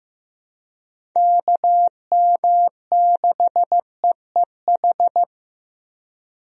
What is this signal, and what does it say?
Morse code, heard as a keyed tone, about 16 wpm, 700 Hz. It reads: KM6EEH
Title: USA Call Signs - 6 Characters - Repeated 2x - 15wpm